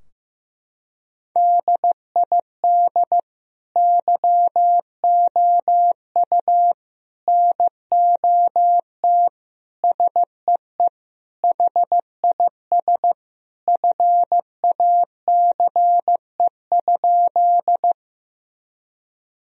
Morse code, heard as a keyed tone, about 16 wpm, 700 Hz. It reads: DID YOU NOT SEE HIS FACE?
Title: Sentences from Top 1000 Words - Review Part 2 - 15wpm